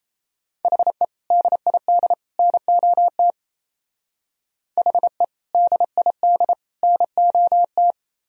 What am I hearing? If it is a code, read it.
Morse code, heard as a keyed tone, about 37 wpm, 700 Hz. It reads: HE DID NOT HE DID NOT